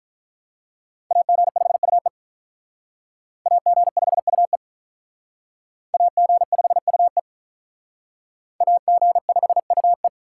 Morse code, heard as a keyed tone, about 45 wpm, 700 Hz. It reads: AG5UE AG5UE AG5UE AG5UE